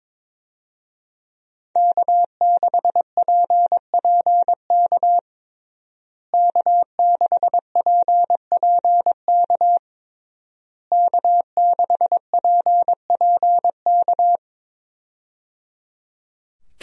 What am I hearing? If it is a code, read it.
Morse code, heard as a keyed tone, about 22 wpm, 700 Hz. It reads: K6PPK K6PPK K6PPK
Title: USA Call Signs - 5 Characters - Repeated 3x - 22wpm